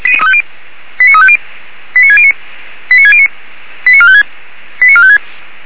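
Five-tone selective call (ZVEI), heard as quick runs of five tones, tones of about 80 ms. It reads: 89350 89609 89609 89456 89456
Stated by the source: ZVEI five-tone call 89293 received by SDR, archived by feuerwehr-orlamuende.org fire brigade server.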